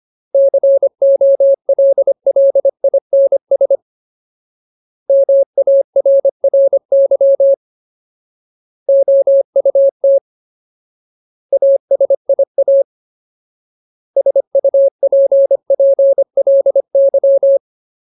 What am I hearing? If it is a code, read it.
Morse code, heard as a keyed tone, about 25 wpm, 550 Hz. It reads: COLLINS MARRY OUT ASIA SUPPLY
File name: Ultimate Word List / 25WPM Ultimate